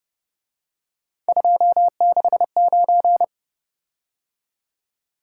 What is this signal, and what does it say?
Morse code, heard as a keyed tone, about 29 wpm, 700 Hz. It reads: 269